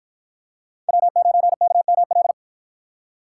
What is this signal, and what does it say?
Morse code, heard as a keyed tone, about 53 wpm, 700 Hz. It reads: W9KML